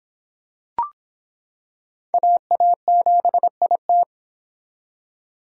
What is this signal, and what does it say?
Morse code, heard as a keyed tone, about 26 wpm, 700 Hz. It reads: AA7IT